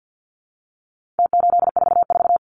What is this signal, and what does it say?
Morse code, heard as a keyed tone, about 50 wpm, 700 Hz. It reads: T844
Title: Numbers - Contest Style - 4-Digit 50wpm